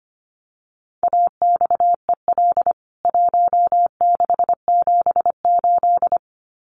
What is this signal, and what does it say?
Morse code, heard as a keyed tone, about 25 wpm, 700 Hz. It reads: AXEL 1678
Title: CWT Exchange 25wpm